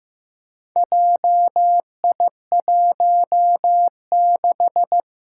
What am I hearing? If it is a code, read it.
Morse code, heard as a keyed tone, about 15 wpm, 700 Hz. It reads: JI16
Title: Four Challenging Letter-Number Combinations 15wpm